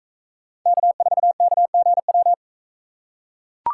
Morse code, heard as a keyed tone, about 41 wpm, 700 Hz. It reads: K4KGW